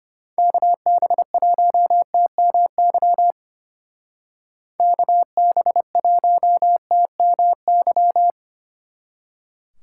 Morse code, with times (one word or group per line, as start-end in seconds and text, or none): 0.4-3.3 KB1TMY
4.8-8.3 KB1TMY